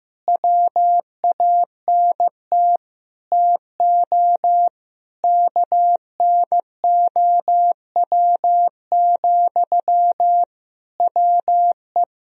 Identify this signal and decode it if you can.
Morse code, heard as a keyed tone, about 15 wpm, 700 Hz. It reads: WANT TO KNOW, WE
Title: Sentences from Top 1000 Words - Review Part 3 - 15wpm